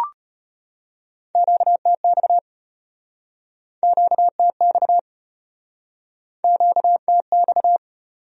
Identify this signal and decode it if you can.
Morse code, heard as a keyed tone, about 36 wpm, 700 Hz. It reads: QTX QTX QTX